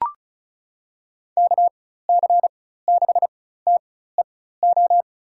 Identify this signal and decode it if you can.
Morse code, heard as a keyed tone, about 36 wpm, spaced as for 16 wpm, 700 Hz. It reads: KC6TEO
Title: USA Call Signs - 6 Characters - 4x Character Spacing - 35wpm